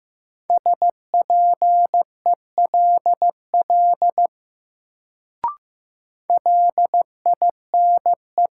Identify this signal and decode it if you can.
Morse code, heard as a keyed tone, about 15 wpm, 700 Hz. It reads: SPELL LINE
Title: Sentences from Top 1000 Words - Review Part 1 - New Word Drill - 15wpm